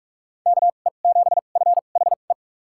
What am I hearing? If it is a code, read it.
Morse code, heard as a keyed tone, about 46 wpm, spaced as for 33 wpm, 700 Hz. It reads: KE7FHE
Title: USA Call Signs - 6 Characters - 2x Character Spacing - 45wpm